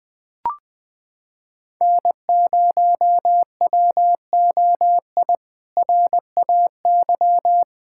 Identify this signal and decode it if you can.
Morse code, heard as a keyed tone, about 20 wpm, 700 Hz. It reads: N0WOI RAY
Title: K1USN SST QSO Practice - 20wpm